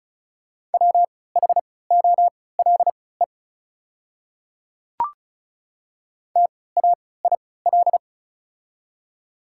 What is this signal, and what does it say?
Morse code, heard as a keyed tone, about 36 wpm, spaced as for 20 wpm, 690 Hz. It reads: WHOLE TAIL
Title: 3x Character Spacing - Top 500 Words - 35wpm